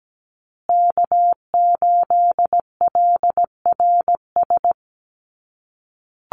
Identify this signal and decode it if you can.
Morse code, heard as a keyed tone, about 17 wpm, 700 Hz. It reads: K8LRS
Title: USA Call Signs Repeated 2x 17wpm